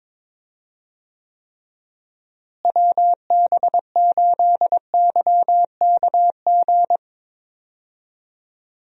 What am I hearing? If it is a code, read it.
Morse code, heard as a keyed tone, about 22 wpm, 700 Hz. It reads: WB8YKG
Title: USA Call Signs - 6 Characters - Repeated 2x - 22wpm